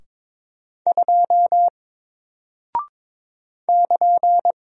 Morse code, heard as a keyed tone, about 22 wpm, 700 Hz. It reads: 2 <KN>
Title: Single Character (Letter:Number:Punctuation:Prosign) Alt 22wpm